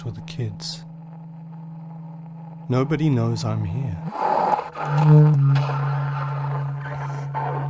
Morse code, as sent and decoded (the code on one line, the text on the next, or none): none